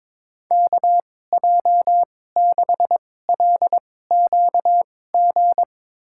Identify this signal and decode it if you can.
Morse code, heard as a keyed tone, about 21 wpm, spaced as for 16 wpm, 700 Hz. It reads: KJ6LQG
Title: USA Call Signs - 6 Characters - 2x Character Spacing - 22wpm